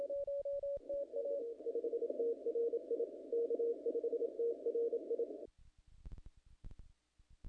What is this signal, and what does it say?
Morse code, heard as a keyed tone, about 27 wpm, 470 Hz. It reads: K5TRI K5TRI
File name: contests / naqp-cw-k5tr-op-k5pi- -1900